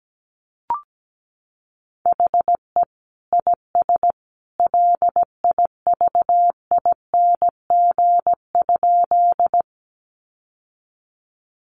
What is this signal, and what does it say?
Morse code, heard as a keyed tone, about 17 wpm, 700 Hz. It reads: HE IS LIVING?
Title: Sentences from Top 1100 Words 17wpm